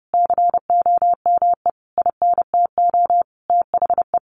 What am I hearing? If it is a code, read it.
Morse code, heard as a keyed tone, about 30 wpm, 700 Hz. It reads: COME INTO THE
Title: Sentences from Top 1200 Words 30wpm